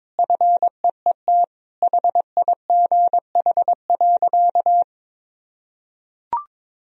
Morse code, heard as a keyed tone, about 22 wpm, 700 Hz. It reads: FEET HIGH.
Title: Sentences from Top 1000 Words - Review Part 2 - 22wpm